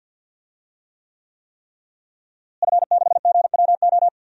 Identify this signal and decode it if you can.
Morse code, heard as a keyed tone, about 52 wpm, 700 Hz. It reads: F6GWO
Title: DXCC-100 Call Signs - 5 Characters - Repeated 3x - 50wpm